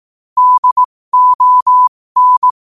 Morse code, heard as a keyed tone, about 18 wpm, spaced as for 15 wpm, 1000 Hz. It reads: DON
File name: qotdfiles / 15wpm